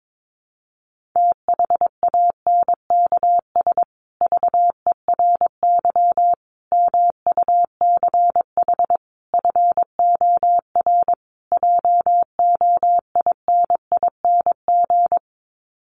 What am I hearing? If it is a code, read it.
Morse code, heard as a keyed tone, about 22 wpm, 700 Hz. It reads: THANKS VERY MUCH FOR JOINING
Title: Sets of 5 Words Encore2 22wpm